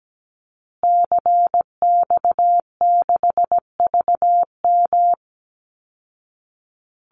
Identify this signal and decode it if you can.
Morse code, heard as a keyed tone, about 17 wpm, 700 Hz. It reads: CX6VM
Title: Call Signs Repeated 2x - 17wpm